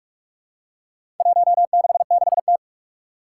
Morse code, heard as a keyed tone, about 44 wpm, 700 Hz. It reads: 166T
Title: Four Challenging Letter-Number Combinations 45wpm